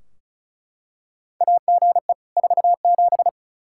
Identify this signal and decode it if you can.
Morse code, heard as a keyed tone, about 34 wpm, 700 Hz. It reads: AGE 47
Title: Practice QSO - Exchange 3 - More Info - 35wpm